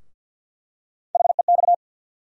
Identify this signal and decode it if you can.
Morse code, heard as a keyed tone, about 52 wpm, 700 Hz. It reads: HEX
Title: USA Call Sign Postfixes - 3 Letters 50wpm